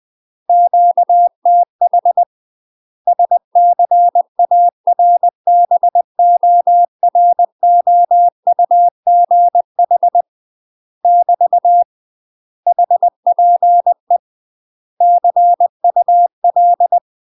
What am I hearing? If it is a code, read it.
Morse code, heard as a keyed tone, about 20 wpm, 700 Hz. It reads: QTH SCARBOROUGH <BT> HPE CUL
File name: UK MASTER QSOs TRAINING / 20WPM / UK MASTER QSOs 0044